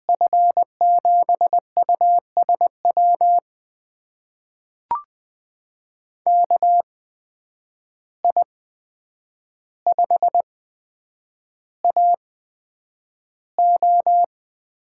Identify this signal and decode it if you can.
Morse code, heard as a keyed tone, about 20 wpm, spaced as for 9 wpm, 700 Hz. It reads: F7USW KI5AO
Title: USA Call Signs - 6 Characters - 8x Character Spacing - 20wpm